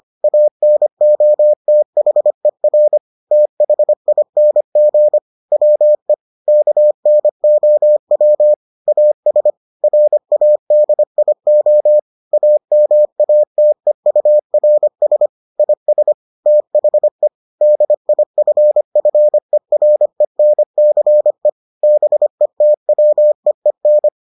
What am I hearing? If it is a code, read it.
Morse code, heard as a keyed tone, about 25 wpm, 600 Hz. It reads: ANOTHER THING WE KNOW AS RADIO AMATEURS IS THE DIFFERENCE BETWEEN